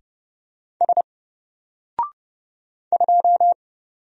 Morse code, heard as a keyed tone, about 29 wpm, 700 Hz. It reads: S 2
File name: Single-Letter-Number-30wpm